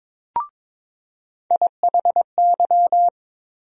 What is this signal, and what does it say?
Morse code, heard as a keyed tone, about 22 wpm, 700 Hz. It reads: IHY
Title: USA Call Sign Postfixes - 3 Letters 22wpm